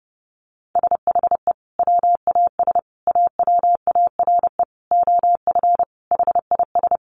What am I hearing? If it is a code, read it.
Morse code, heard as a keyed tone, about 30 wpm, 700 Hz. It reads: SHE WAS AWARE OF HIS
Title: Sentences from Top 1200 Words 30wpm